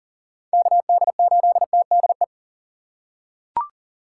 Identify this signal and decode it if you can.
Morse code, heard as a keyed tone, about 40 wpm, 700 Hz. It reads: KD8TDE
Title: USA Call Signs - 2x Character Spacing - 40wpm